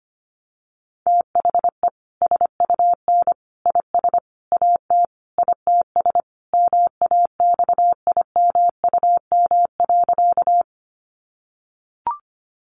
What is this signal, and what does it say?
Morse code, heard as a keyed tone, about 25 wpm, 700 Hz. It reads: THE SUN IS AT ITS MAXIMUM.
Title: Sentences from Top 1300 Words 25wpm